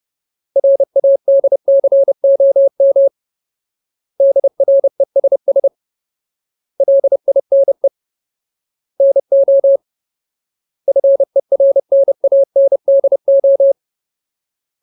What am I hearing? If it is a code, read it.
Morse code, heard as a keyed tone, about 30 wpm, 550 Hz. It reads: RADCOM DRESS LINE NO FERNANDO
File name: Ultimate Word List / 30WPM Ultimate